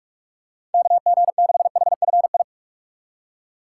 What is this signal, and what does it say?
Morse code, heard as a keyed tone, about 42 wpm, 700 Hz. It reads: KG6HFI
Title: USA Call Signs - 4x Character Spacing - 45wpm